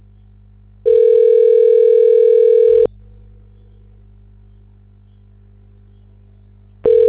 Ringback tone (North America), 440 + 480 Hz.
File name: recordings / REe d2650b39eb e91